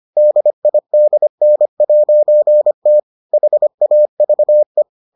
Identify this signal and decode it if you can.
Morse code, heard as a keyed tone, about 25 wpm, 600 Hz. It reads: DIDN'T HAVE